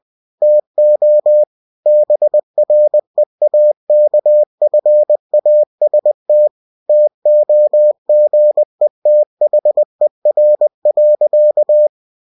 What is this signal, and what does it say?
Morse code, heard as a keyed tone, about 20 wpm, 600 Hz. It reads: TO BREAKFAST TOGETHER.
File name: cw-books / winnie-the-pooh / winnie-20wpm-page-0043